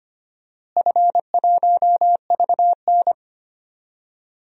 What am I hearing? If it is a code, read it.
Morse code, heard as a keyed tone, about 25 wpm, 700 Hz. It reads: F1VN
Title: DXCC-100 Call Signs - 4 Characters - Repeated 3x - 25wpm